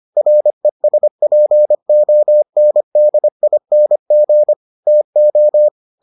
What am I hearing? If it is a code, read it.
Morse code, heard as a keyed tone, about 25 wpm, 600 Hz. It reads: RESPONDING TO